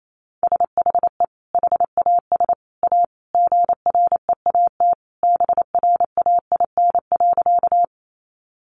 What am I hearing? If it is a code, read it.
Morse code, heard as a keyed tone, about 28 wpm, 700 Hz. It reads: SHE HAS A GREAT BRAIN.
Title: Sentences from Top 1100 Words 28wpm